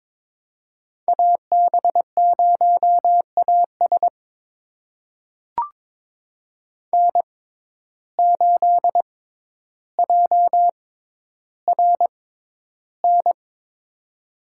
Morse code, heard as a keyed tone, about 22 wpm, spaced as for 11 wpm, 700 Hz. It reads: AB0AS N8JRN
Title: USA Call Signs - 5 Characters - 6x Character Spacing - 22wpm